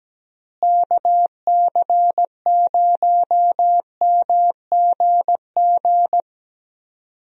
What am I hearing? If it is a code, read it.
Morse code, heard as a keyed tone, about 17 wpm, 700 Hz. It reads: KC0MGG
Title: USA Call Signs - 6 Characters - Repeated 2x - 17wpm